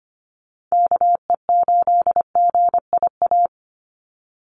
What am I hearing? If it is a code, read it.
Morse code, heard as a keyed tone, about 25 wpm, 700 Hz. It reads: KE8GIA
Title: USA Call Signs 25wpm